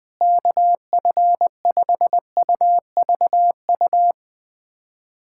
Morse code, heard as a keyed tone, about 19 wpm, 700 Hz. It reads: KF5UVU
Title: USA Callsign DE Callsign - 6 Characters - 20wpm